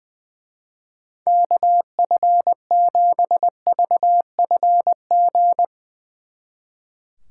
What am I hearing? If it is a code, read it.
Morse code, heard as a keyed tone, about 20 wpm, 700 Hz. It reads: KF7VFG
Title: Speed Racing - USA 2x3 Call Signs 20wpm